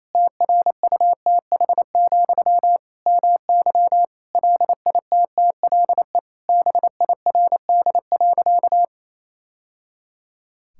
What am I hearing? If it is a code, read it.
Morse code, heard as a keyed tone, about 28 wpm, 700 Hz. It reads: TRUTH, MY LITTLE BIRD.
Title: Sentences from Top 1000 Words - Review Part 6 - 28wpm